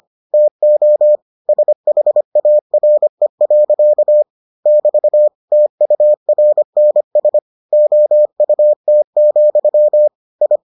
Morse code, heard as a keyed tone, about 25 wpm, 600 Hz. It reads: TO SHARE. <BT> TURNS OUT, I